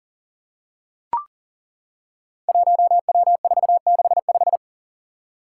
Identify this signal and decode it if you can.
Morse code, heard as a keyed tone, about 38 wpm, 700 Hz. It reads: 1W465
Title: Five Challenging Letter-Number Combinations 40wpm